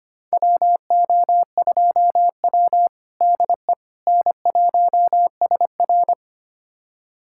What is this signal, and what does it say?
Morse code, heard as a keyed tone, about 25 wpm, 700 Hz. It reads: WO2W DE N1SR
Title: USA Callsign DE Callsign - 4 Characters - 25wpm